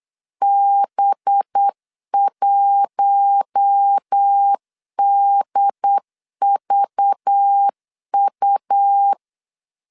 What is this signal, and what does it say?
Morse code, heard as a keyed tone, about 8 wpm, 800 Hz. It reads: B1DVU
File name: CW / DET / 1 6 1500C S B 8wpm 5142